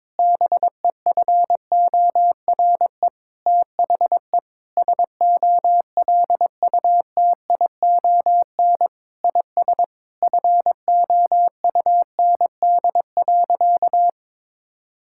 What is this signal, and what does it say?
Morse code, heard as a keyed tone, about 22 wpm, 700 Hz. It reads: BEFORE THE SOLUTION IS FOUND.
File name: Sentences with 9-Word Phrases - 22wpm